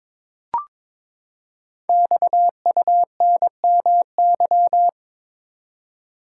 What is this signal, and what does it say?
Morse code, heard as a keyed tone, about 22 wpm, 700 Hz. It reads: XUNMY
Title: Code Groups - 5 Letters - 22wpm